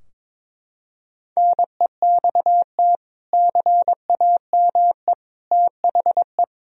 Morse code, heard as a keyed tone, about 22 wpm, 700 Hz. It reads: NEXT CAME THE
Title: Sentences from Top 1000 Words - Review Part 4 - 22wpm